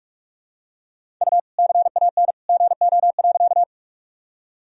Morse code, heard as a keyed tone, about 46 wpm, 700 Hz. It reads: U CAN GO.
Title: Sentences from Top 100 Words and Abbreviations 45wpm